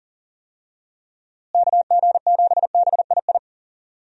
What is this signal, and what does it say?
Morse code, heard as a keyed tone, about 40 wpm, 700 Hz. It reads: KG7BII
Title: USA Call Signs - 6 Characters - Repeated 2x - 40wpm